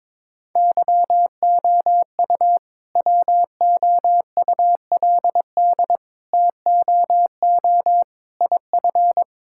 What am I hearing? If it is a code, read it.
Morse code, heard as a keyed tone, about 22 wpm, 690 Hz. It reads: YOU WOULD TOO IF
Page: morse-code-ninja-practice-22wpm